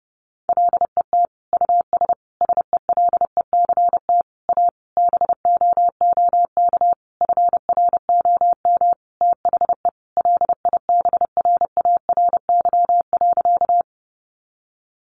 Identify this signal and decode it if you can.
Morse code, heard as a keyed tone, about 30 wpm, 700 Hz. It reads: LET US SELECT A BOOK FROM THE LIBRARY.
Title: Sentences from Top 1300 Words 30wpm